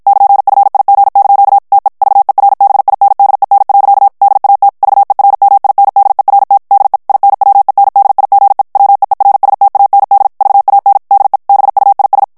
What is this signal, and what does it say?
Morse code, heard as a keyed tone, about 56 wpm, 800 Hz. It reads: <KN>FIG) TE VERBINDEN. DAT VERMINDERT DE INWENDIGE WEERSTAND VAN DE BUIS
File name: HOLLANDAIS texte clair 2 vitesse 50 mots minute